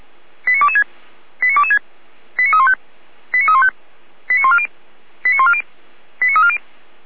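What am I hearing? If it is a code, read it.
Five-tone selective call (ZVEI), heard as quick runs of five tones, tones of about 70 ms. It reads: 89226 89226 89315 89315 89140 89140 89340
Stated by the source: ZVEI five-tone selective call received on an SDR.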